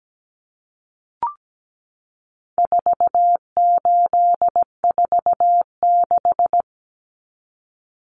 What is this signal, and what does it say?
Morse code, heard as a keyed tone, about 17 wpm, 700 Hz. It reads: 4846
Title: Common Cut Numbers - 4-Digit 17wpm